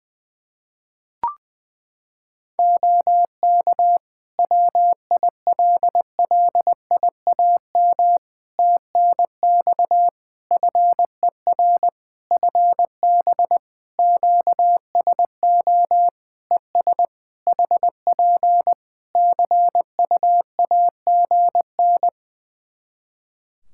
Morse code, heard as a keyed tone, about 20 wpm, 700 Hz. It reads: OK WILLIAM TNX FER FB QSO ES HP CUAGN
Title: Practice QSO - Exchange 4 - Ending - 20wpm